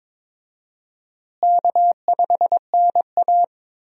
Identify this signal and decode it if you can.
Morse code, heard as a keyed tone, about 22 wpm, 700 Hz. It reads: K5NA
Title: USA Call Signs - 4 Characters - Repeated 3x - 22wpm